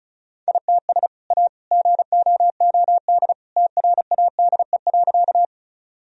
Morse code, heard as a keyed tone, about 34 wpm, 700 Hz. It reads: ITS A GOOD TRADE.
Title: Sentences from Top 1000 Words - Review Part 4 - 35wpm